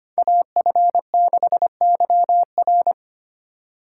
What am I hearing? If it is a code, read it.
Morse code, heard as a keyed tone, about 30 wpm, 700 Hz. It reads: AF6YR
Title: Speed Racing - USA Call Signs 25wpm